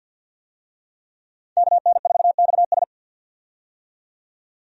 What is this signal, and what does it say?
Morse code, heard as a keyed tone, about 50 wpm, 700 Hz. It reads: KN4XS
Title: USA Call Signs - 5 Characters - 50wpm